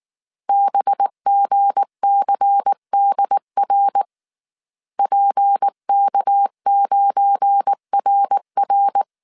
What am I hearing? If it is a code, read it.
Morse code, heard as a keyed tone, about 19 wpm, 800 Hz. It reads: BGCDR PK9RR
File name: CW / DET / 1 9 1500C S A 18wpm 2459